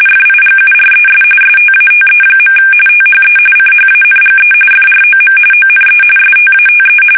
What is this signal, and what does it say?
RTTY, about 45.45 baud. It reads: FIGURES STATE. THE CHANGE FROM ONE STATE
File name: fsk850 / teletype-RTTY